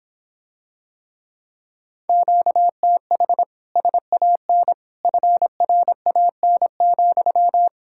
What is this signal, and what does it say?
Morse code, heard as a keyed tone, about 26 wpm, 700 Hz. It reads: QTH SAN FRAN,